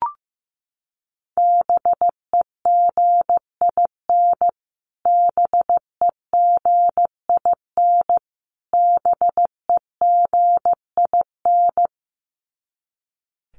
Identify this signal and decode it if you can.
Morse code, heard as a keyed tone, about 14 wpm, 700 Hz. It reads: BEGIN BEGIN BEGIN
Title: Top 500 Words Repeated 15wpm